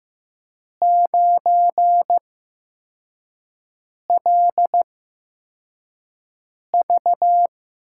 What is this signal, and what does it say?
Morse code, heard as a keyed tone, about 15 wpm, spaced as for 4 wpm, 700 Hz. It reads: 9LV
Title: USA Call Signs - 6 Characters - 8x Character Spacing - 15wpm